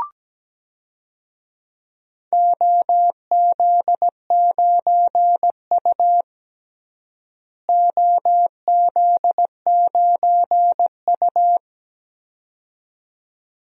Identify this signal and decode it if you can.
Morse code, heard as a keyed tone, about 17 wpm, 700 Hz. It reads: OZ9U OZ9U
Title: DXCC-100 Call Signs - 4 Characters - Repeated 2x - 17wpm